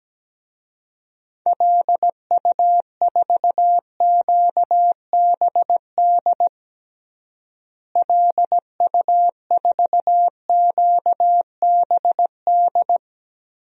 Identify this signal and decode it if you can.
Morse code, heard as a keyed tone, about 17 wpm, 700 Hz. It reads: LU4QBD LU4QBD
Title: DXCC-100 Call Signs - 6 Characters - Repeated 3x - 17wpm